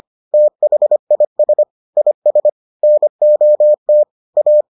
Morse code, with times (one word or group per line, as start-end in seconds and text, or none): 0.3-1.6 THIS
2.0-2.5 IS
2.8-4.0 NOT
4.4-4.6 A